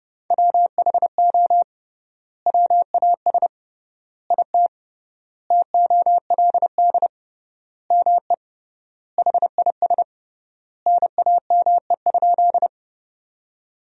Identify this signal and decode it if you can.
Morse code, heard as a keyed tone, about 30 wpm, 700 Hz. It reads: WHO WAS IT TOLD ME HIS NAME?
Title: 3x Word-Spacing - Sentences from Top 400 Words - 30wpm